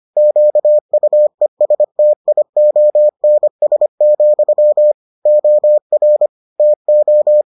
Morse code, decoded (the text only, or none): QUESTIONS, OR TO